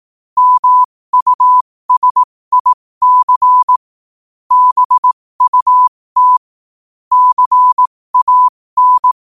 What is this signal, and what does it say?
Morse code, heard as a keyed tone, about 18 wpm, spaced as for 15 wpm, 1000 Hz. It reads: MUSIC BUT CAN
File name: qotdfiles / 15wpm